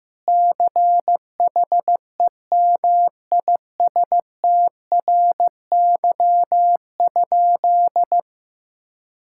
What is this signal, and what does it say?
Morse code, heard as a keyed tone, about 16 wpm, 700 Hz. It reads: CHEMISTRY?
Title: Sentences from Top 1000 Words - Review Part 6 - 15wpm